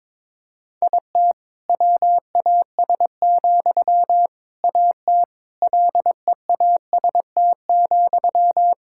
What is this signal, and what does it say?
Morse code, heard as a keyed tone, about 22 wpm, 700 Hz. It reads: IT WAS, AT LEAST,
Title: Sentences from Top 1000 Words - Review Part 6 - 22wpm